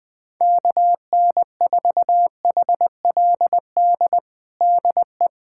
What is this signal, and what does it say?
Morse code, heard as a keyed tone, about 20 wpm, 700 Hz. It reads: KN4HLD DE
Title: USA Callsign DE Callsign 20wpm